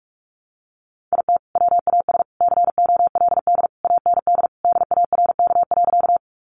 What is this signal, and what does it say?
Morse code, heard as a keyed tone, about 44 wpm, 700 Hz. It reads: IT WAS COLD AND DARK.
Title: Sentences from Top 500 Words 45wpm